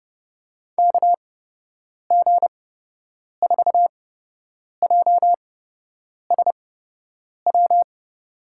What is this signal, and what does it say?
Morse code, heard as a keyed tone, about 31 wpm, spaced as for 8 wpm, 700 Hz. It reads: KG4JSW
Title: USA Call Signs - 8x Character Spacing - 30wpm